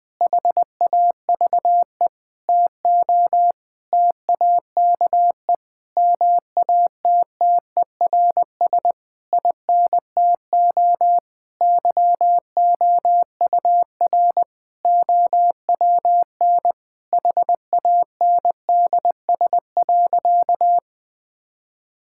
Morse code, heard as a keyed tone, about 19 wpm, 700 Hz. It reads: HAVE TO TAKE MATTERS INTO YOUR OWN HANDS.
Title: Sentences with 7-Word Phrases - Spoken Beginning - 20wpm